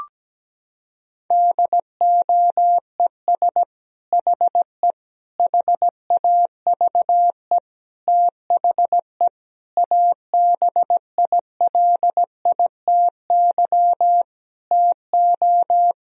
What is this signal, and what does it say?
Morse code, heard as a keyed tone, about 17 wpm, 700 Hz. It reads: DOES HE HAVE THE ABILITY TO
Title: Sentences from Top 1000 Words - Review Part 3 - 17wpm